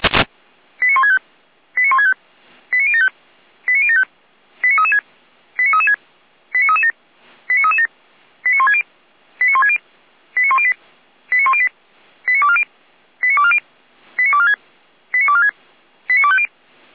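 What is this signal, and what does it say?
Five-tone selective call (ZVEI), heard as quick runs of five tones, tones of about 70 ms. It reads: 89165 89165 89075 89075 89336 89336 89337 89337 89160 89160 89108 89108 89240 89240 89246 89246 89250
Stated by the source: ZVEI five-tone call received by SDR, from the feuerwehr-orlamuende.org (fire brigade Orlamünde) audio server.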